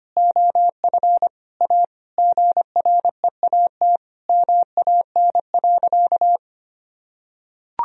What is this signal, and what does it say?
Morse code, heard as a keyed tone, about 25 wpm, 700 Hz. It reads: OF A GREAT MAN.
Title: Sentences from Top 1000 Words - Review Part 2 - 25wpm